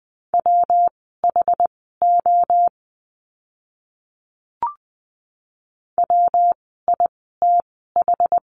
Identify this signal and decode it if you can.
Morse code, heard as a keyed tone, about 20 wpm, spaced as for 14 wpm, 700 Hz. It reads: WHO WITH
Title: 199 Lesson 32f - Letter k - Words 20wpm